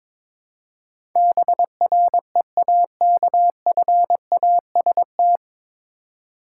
Morse code, heard as a keyed tone, about 22 wpm, 700 Hz. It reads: BREAKFAST
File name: 9 Letter Words - First 3 Letters Given - 22wpm